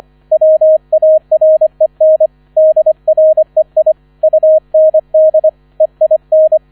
Morse code, heard as a keyed tone, about 25 wpm, 630 Hz, with radio noise. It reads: WAREN DREI UND EIN